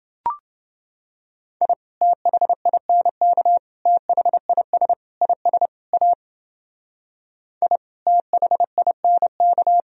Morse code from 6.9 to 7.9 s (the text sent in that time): I